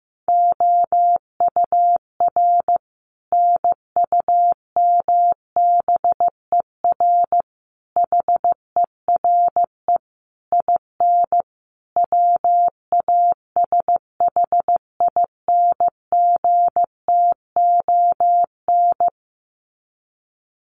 Morse code, heard as a keyed tone, about 15 wpm, 700 Hz. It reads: OUR NUMBER HERE IN WASHINGTON
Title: Sets of 5 Words Encore2 15wpm